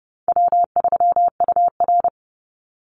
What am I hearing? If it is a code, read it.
Morse code, heard as a keyed tone, about 30 wpm, 700 Hz. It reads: W3UR
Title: Easy Call Signs Repeated 2x - 30wpm